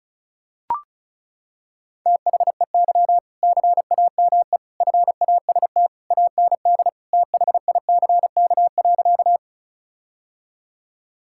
Morse code, heard as a keyed tone, about 34 wpm, 700 Hz. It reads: THEY CAME FAST AND THICK.
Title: Sentences from Top 1600 Words 35wpm